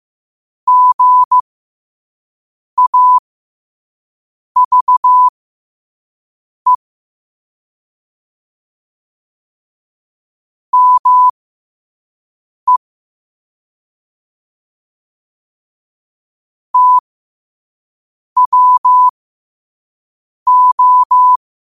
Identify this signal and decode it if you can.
Morse code, heard as a keyed tone, about 15 wpm, spaced as for 5 wpm, 1000 Hz. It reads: GAVE ME TWO
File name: qotdfiles / 5wpm